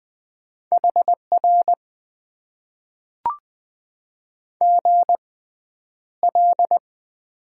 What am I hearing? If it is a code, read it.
Morse code, heard as a keyed tone, about 20 wpm, spaced as for 10 wpm, 700 Hz. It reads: HR GL
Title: Top 90 QSO Elements - 6x Character Spacing - 20wpm